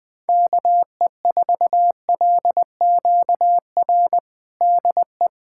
Morse code, heard as a keyed tone, about 20 wpm, 700 Hz. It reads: KE4LQR DE
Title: USA Callsign DE Callsign - 6 Characters - 20wpm